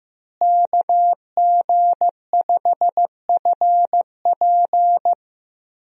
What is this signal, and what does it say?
Morse code, heard as a keyed tone, about 15 wpm, 700 Hz. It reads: KG5FP
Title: USA Call Signs - 5 Characters - Repeated 3x - 15wpm